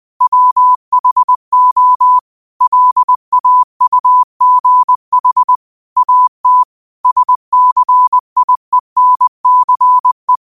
Morse code, heard as a keyed tone, about 20 wpm, 1000 Hz. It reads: WHO LAUGH AT SCIENCE
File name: qotdfiles / 20wpm